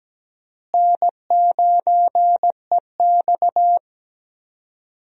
Morse code, heard as a keyed tone, about 17 wpm, 700 Hz. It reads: N9EX
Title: USA Call Signs - 4 Characters - Repeated 3x - 17wpm